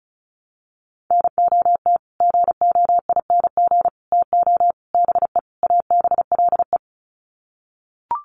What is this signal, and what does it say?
Morse code, heard as a keyed tone, about 34 wpm, 700 Hz. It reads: NOT GOING TO BE ABLE
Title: Sets of 5 Words 35wpm